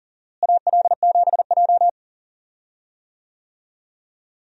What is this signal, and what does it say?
Morse code, heard as a keyed tone, about 37 wpm, 700 Hz. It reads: AL7J
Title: USA Call Signs - 4 Characters - Repeated 2x - 40wpm